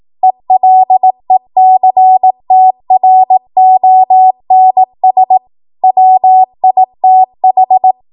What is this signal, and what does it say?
Morse code, heard as a keyed tone, about 18 wpm, 750 Hz. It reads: ELECTRONS WITH
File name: Morse / 18WPM